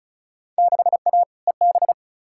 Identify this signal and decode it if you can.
Morse code, heard as a keyed tone, about 36 wpm, 700 Hz. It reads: 6A EB
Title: Field Day Exchanges Encore - 35wpm